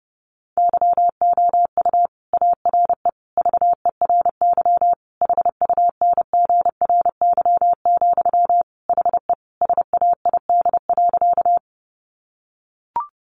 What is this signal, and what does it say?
Morse code, heard as a keyed tone, about 30 wpm, 700 Hz. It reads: YOU ARE VERY HUNGRY, HE SAID.
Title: Sentences from Top 1400 Words 30wpm